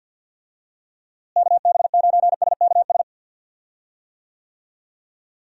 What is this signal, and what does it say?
Morse code, heard as a keyed tone, about 49 wpm, 700 Hz. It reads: KB9SKS
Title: USA Call Signs - 6 Characters - Repeated 2x - 50wpm